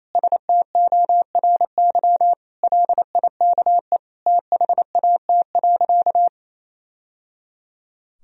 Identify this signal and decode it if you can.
Morse code, heard as a keyed tone, about 28 wpm, 700 Hz. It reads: STORY LIKE THAT.
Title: Sentences from Top 1000 Words - Review Part 3 - 28wpm